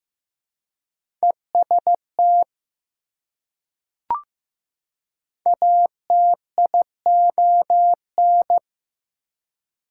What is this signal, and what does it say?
Morse code, heard as a keyed tone, about 15 wpm, 700 Hz. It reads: EST ATION
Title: Prefixes and Suffixes 15wpm